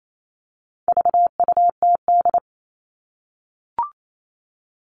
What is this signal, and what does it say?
Morse code, heard as a keyed tone, about 28 wpm, 700 Hz. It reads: VUTD
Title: All Cut Numbers - 4-Digit 28wpm